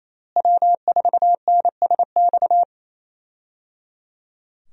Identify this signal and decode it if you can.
Morse code, heard as a keyed tone, about 28 wpm, 700 Hz. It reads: W4NSX
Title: USA Call Signs - 5 Characters - Repeated 3x - 28wpm